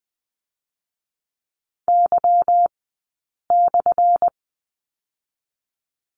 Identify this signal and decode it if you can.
Morse code, heard as a keyed tone, about 20 wpm, spaced as for 8 wpm, 690 Hz. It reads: Y/
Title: 217 Lesson 35d - Number 0 - Two Characters 20wpm